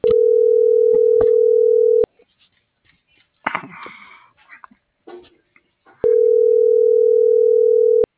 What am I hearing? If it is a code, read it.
Ringback tone (North America), 440 + 480 Hz.